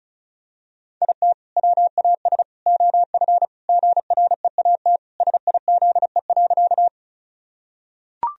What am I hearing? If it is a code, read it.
Morse code, heard as a keyed tone, about 34 wpm, 700 Hz. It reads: IT WAS OF GREAT SIZE.
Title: Sentences from Top 1000 Words - Review Part 2 - 35wpm